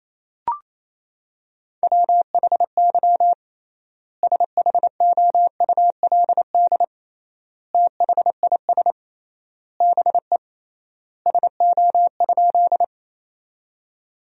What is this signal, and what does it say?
Morse code, heard as a keyed tone, about 28 wpm, 700 Hz. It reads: WHY SHOULD THIS BE SO?
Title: 3x Word-Spacing - Sentences from Top 300 Words - 28wpm